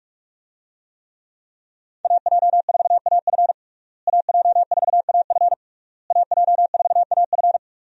Morse code, heard as a keyed tone, about 48 wpm, 700 Hz. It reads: AJ4AF AJ4AF AJ4AF
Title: USA Call Signs - 5 Characters - Repeated 3x - 45wpm